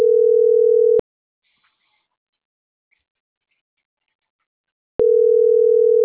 Ringback tone (North America), 440 + 480 Hz.